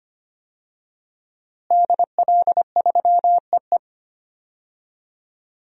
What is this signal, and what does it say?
Morse code, heard as a keyed tone, about 25 wpm, 700 Hz. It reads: DL3EE
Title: DXCC-100 Call Signs - 5 Characters - Repeated 3x - 25wpm